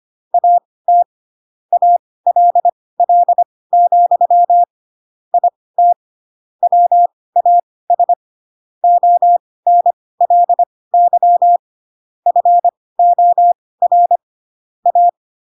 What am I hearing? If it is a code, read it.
Morse code, heard as a keyed tone, about 25 wpm, spaced as for 18 wpm, 700 Hz. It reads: AT ALL, IT WAS ONLY FOR A